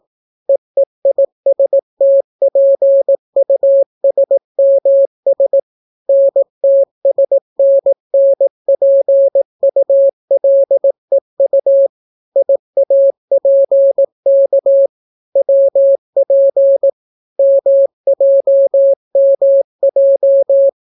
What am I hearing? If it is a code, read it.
Morse code, heard as a keyed tone, about 17 wpm, 550 Hz. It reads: EEISTPUSMS NTSNNPULEU IAPK WP MJMJ